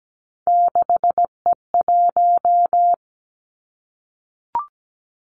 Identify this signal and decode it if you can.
Morse code, heard as a keyed tone, about 17 wpm, 700 Hz. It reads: 6E1
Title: Three Challenging Letter-Number Combinations 17wpm